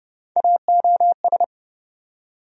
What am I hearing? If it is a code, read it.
Morse code, heard as a keyed tone, about 30 wpm, 700 Hz. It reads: AOS
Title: USA Call Sign Postfixes - 3 Letters 30wpm